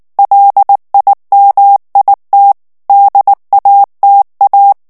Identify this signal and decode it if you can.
Morse code, heard as a keyed tone, about 19 wpm, 800 Hz. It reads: LIMIT DATA
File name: ANGLAIS texte clair 3 vitesse 18 mots minute